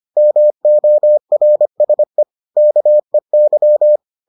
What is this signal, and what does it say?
Morse code, heard as a keyed tone, about 25 wpm, 600 Hz. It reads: MORSE KEY